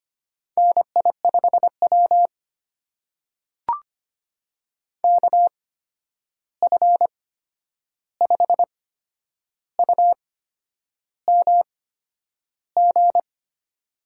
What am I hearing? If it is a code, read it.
Morse code, heard as a keyed tone, about 25 wpm, spaced as for 9 wpm, 700 Hz. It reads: NI5W KF5UMG
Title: USA Call Signs - 8x Character Spacing - 25wpm